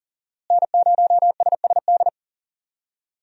Morse code, heard as a keyed tone, about 40 wpm, 700 Hz. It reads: N0SSD
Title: USA Call Signs - 5 Characters - Repeated 3x - 40wpm